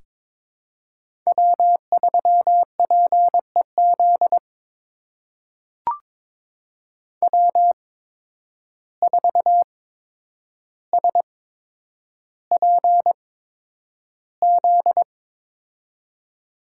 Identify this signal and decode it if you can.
Morse code, heard as a keyed tone, about 21 wpm, spaced as for 9 wpm, 700 Hz. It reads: W3PEZ W4SPZ
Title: USA Call Signs - 5 Characters - 8x Character Spacing - 22wpm